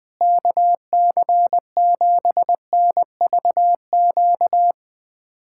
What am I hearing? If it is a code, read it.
Morse code, heard as a keyed tone, about 20 wpm, 700 Hz. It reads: KC7NVQ
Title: USA Callsign DE Callsign 20wpm